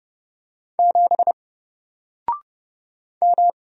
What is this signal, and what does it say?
Morse code, heard as a keyed tone, about 30 wpm, 700 Hz. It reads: 7 M
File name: Single-Letter-Number-30wpm